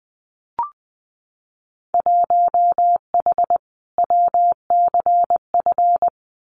Morse code, heard as a keyed tone, about 20 wpm, 700 Hz. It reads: 1H WCF
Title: Winter Field Day Exchanges 20wpm